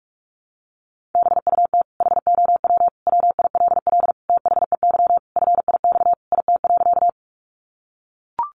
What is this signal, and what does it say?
Morse code, heard as a keyed tone, about 44 wpm, 700 Hz. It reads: BUT HOW WILL THEY FIX IT.
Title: Sentences from Top 1400 Words 45wpm